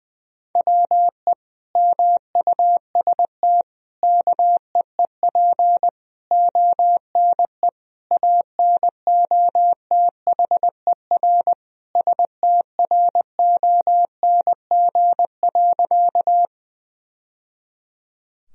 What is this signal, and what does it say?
Morse code, heard as a keyed tone, about 20 wpm, 700 Hz. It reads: WE MUST KEEP ONE ANOTHER STRONG.
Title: Sentences from Top 1000 Words - Review Part 3 - 20wpm